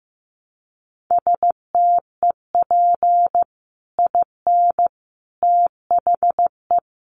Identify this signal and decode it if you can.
Morse code, heard as a keyed tone, about 15 wpm, 700 Hz. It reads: STEP IN THE
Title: Sets of 5 Words Encore 15wpm